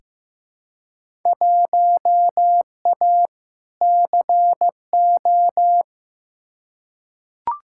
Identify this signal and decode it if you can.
Morse code, heard as a keyed tone, about 15 wpm, 700 Hz. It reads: 1A CO
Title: Field Day Exchanges - 15wpm